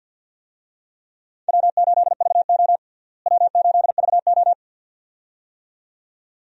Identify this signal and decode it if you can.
Morse code, heard as a keyed tone, about 49 wpm, 700 Hz. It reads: W8VO W8VO
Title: USA Call Signs - 4 Characters - Repeated 2x - 50wpm